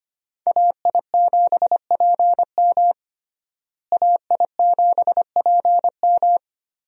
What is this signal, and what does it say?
Morse code, heard as a keyed tone, about 25 wpm, 700 Hz. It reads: AI7PM AI7PM
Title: USA Call Signs - 5 Characters - Repeated 3x - 25wpm